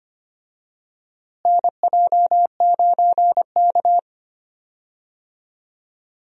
Morse code, heard as a keyed tone, about 25 wpm, 700 Hz. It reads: NJ9K
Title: USA Call Signs - 4 Characters - Repeated 2x - 25wpm